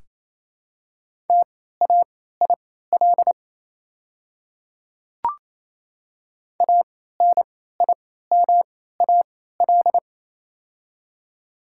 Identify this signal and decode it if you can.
Morse code, heard as a keyed tone, about 28 wpm, spaced as for 16 wpm, 690 Hz. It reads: TAIL ANIMAL
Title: 3x Character Spacing - Top 500 Words - 28wpm